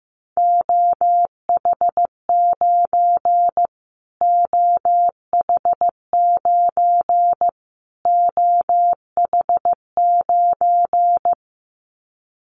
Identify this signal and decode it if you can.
Morse code, heard as a keyed tone, about 15 wpm, 700 Hz. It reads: OH9 OH9 OH9
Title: Callsign Prefixes for Top 25 DXCC - Repeated 3x - 15wpm